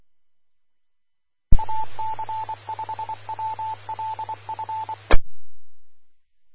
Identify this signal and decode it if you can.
Morse code, heard as a keyed tone, about 24 wpm, 880 Hz. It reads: AC5WLF
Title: Austin, TX Amateur Repeaters